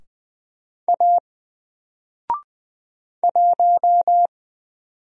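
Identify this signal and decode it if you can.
Morse code, heard as a keyed tone, about 20 wpm, 700 Hz. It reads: A 1
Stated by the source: Single letter and number Morse drill at 20 wpm, from N4LCD.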